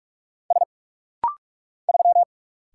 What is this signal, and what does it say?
Morse code, heard as a keyed tone, about 44 wpm, 700 Hz. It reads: S 3
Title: Single Letter-Number - Rapid-Fire 45wpm